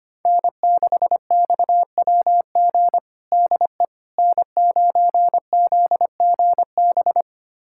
Morse code, heard as a keyed tone, about 25 wpm, 700 Hz. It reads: N6XWG DE N9ZGB
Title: USA Callsign DE Callsign - 5 Characters - 25wpm